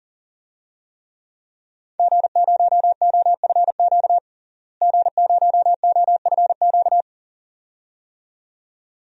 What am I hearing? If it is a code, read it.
Morse code, heard as a keyed tone, about 40 wpm, 700 Hz. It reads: G0OFQ G0OFQ
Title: DXCC-100 Call Signs - 5 Characters - Repeated 2x - 40wpm